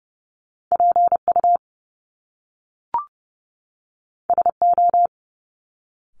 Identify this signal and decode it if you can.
Morse code, heard as a keyed tone, about 30 wpm, 700 Hz. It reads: PU SO
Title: Top Two Letter Combinations in Words 30wpm